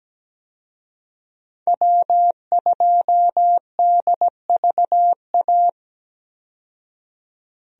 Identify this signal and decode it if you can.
Morse code, heard as a keyed tone, about 17 wpm, 700 Hz. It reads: W2DVA
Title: USA Call Signs - 5 Characters - 17wpm